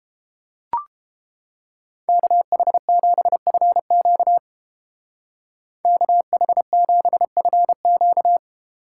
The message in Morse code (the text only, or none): KH7FQ KH7FQ